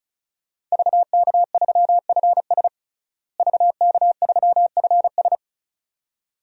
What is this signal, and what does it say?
Morse code, heard as a keyed tone, about 34 wpm, 700 Hz. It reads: VK3FS VK3FS
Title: USA Call Signs - 5 Characters - Repeated 2x - 35wpm